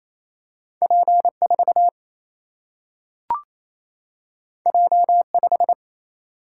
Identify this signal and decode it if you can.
Morse code, heard as a keyed tone, about 28 wpm, 700 Hz. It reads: P 4 J 5
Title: Single-Letter Single-Number 28wpm